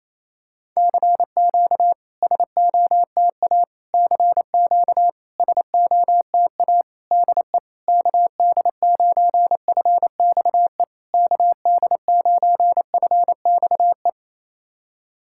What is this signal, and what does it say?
Morse code, heard as a keyed tone, about 28 wpm, 690 Hz. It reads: CQ SOTA CQ SOTA DE KD9FXE KD9FXE
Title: SOTA QSO Practice 28wpm